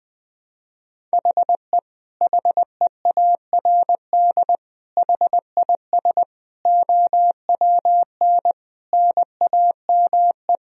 Morse code, heard as a keyed tone, about 20 wpm, 700 Hz. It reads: HE HEARD HIS OWN NAME
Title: Sentences from Top 1000 Words - Review Part 2 - 20wpm